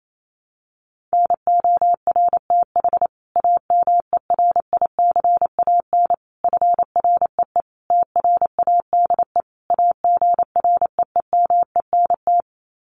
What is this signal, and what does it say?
Morse code, heard as a keyed tone, about 28 wpm, 700 Hz. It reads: NORTH AMERICAN FREE TRADE AGREEMENT
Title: Sets of 5 Words Encore 28wpm